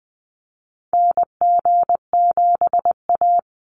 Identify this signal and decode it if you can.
Morse code, heard as a keyed tone, about 20 wpm, 700 Hz. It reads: NG7A
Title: Call Signs Repeated 3x - 20wpm